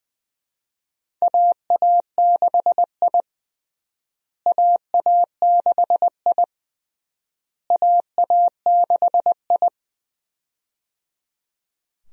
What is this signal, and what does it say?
Morse code, heard as a keyed tone, about 21 wpm, 700 Hz. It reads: AA6I AA6I AA6I
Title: USA Call Signs - 4 Characters - Repeated 3x - 20wpm